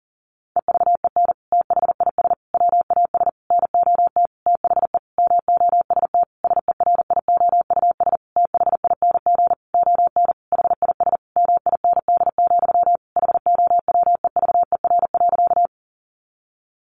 Morse code, heard as a keyed tone, about 40 wpm, 700 Hz. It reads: EVEN THIS WAS NOT THE MOST SERIOUS THING ON HIS MIND, HOWEVER.
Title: Sentences from Top 1100 Words 40wpm